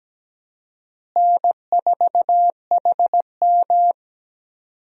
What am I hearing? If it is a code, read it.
Morse code, heard as a keyed tone, about 17 wpm, 700 Hz. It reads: N4HM
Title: USA Call Signs - 4 Characters - Repeated 3x - 17wpm